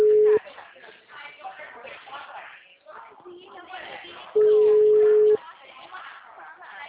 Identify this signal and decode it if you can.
Ringback tone (Europe, 425 Hz), with speech.